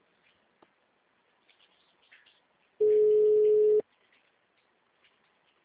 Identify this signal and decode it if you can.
Ringback tone (Europe, 425 Hz).